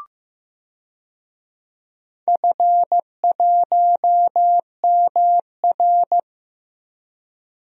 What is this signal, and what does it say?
Morse code, heard as a keyed tone, about 15 wpm, 700 Hz. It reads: F1MR
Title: DXCC-100 Call Signs - 4 Characters - Repeated 2x - 15wpm